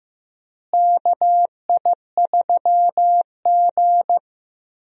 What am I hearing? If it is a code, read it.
Morse code, heard as a keyed tone, about 15 wpm, 700 Hz. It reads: KI3G
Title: USA Call Signs - 4 Characters - Repeated 3x - 15wpm